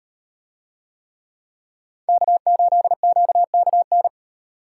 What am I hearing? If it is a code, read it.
Morse code, heard as a keyed tone, about 38 wpm, 700 Hz. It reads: K8QKN